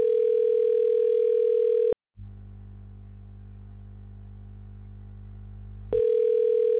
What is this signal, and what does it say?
Ringback tone (North America), 440 + 480 Hz.
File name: recordings / REc819ad44f8549d723d412de5a82cbeb1